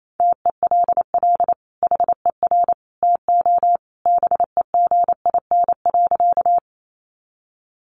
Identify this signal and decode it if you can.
Morse code, heard as a keyed tone, about 28 wpm, 700 Hz. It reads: TELL HER TO BEGIN.
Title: Sentences from Top 300 Words 28wpm